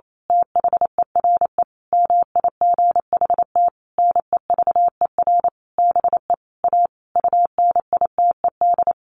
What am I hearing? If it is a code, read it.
Morse code, heard as a keyed tone, about 28 wpm, 700 Hz. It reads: THERE MIGHT NEVER BE A UNITED
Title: A Sound Of Thunder 28wpm